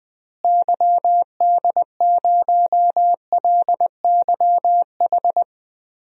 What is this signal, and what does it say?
Morse code, heard as a keyed tone, about 20 wpm, 700 Hz. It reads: YD0LYH
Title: DXCC-100 Call Signs - 6 Characters - Repeated 2x - 20wpm